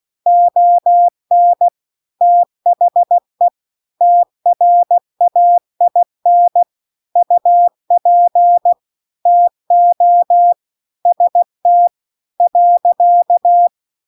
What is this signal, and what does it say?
Morse code, heard as a keyed tone, about 16 wpm, 700 Hz. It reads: ON THE TRAIN UP TO ST .